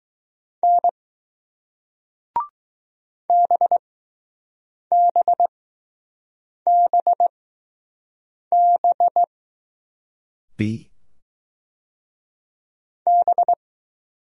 Morse code, heard as a keyed tone, about 20 wpm, spaced as for 6 wpm, 700 Hz. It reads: N BBBB B